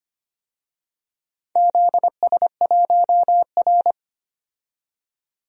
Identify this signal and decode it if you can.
Morse code, heard as a keyed tone, about 25 wpm, 700 Hz. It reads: ZS1R
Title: DXCC-100 Call Signs - 4 Characters - 25wpm